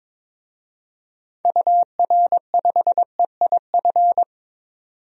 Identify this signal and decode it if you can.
Morse code, heard as a keyed tone, about 21 wpm, 700 Hz. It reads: UR5EIF
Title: DXCC-100 Call Signs - 6 Characters - 22wpm